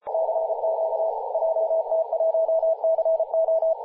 Morse code, heard as a keyed tone, about 34 wpm, 650 Hz, with radio noise. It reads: CQ JT1CO
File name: posts / 2016-july-17-0522z-jt1co-50-098mhz-cw